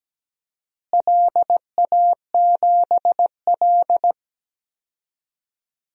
Morse code, heard as a keyed tone, about 17 wpm, 700 Hz. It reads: LA7L
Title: DXCC-100 Call Signs - 4 Characters - Repeated 2x - 17wpm